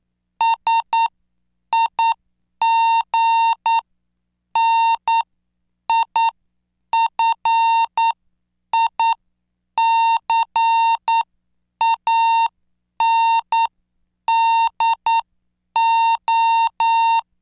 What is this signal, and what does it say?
Morse code, heard as a keyed tone, about 9 wpm, spaced as for 7 wpm, 910 Hz. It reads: SIGNIFICANDO